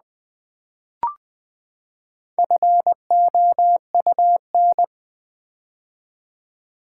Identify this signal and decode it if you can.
Morse code, heard as a keyed tone, about 20 wpm, 700 Hz. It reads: FOUN
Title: Build A Word - 4 Letters At A Time - 10-Letter Words - 20wpm